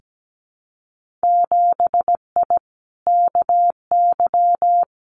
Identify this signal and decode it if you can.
Morse code, heard as a keyed tone, about 17 wpm, 700 Hz. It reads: 7I KY
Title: Winter Field Day Exchanges Encore 17wpm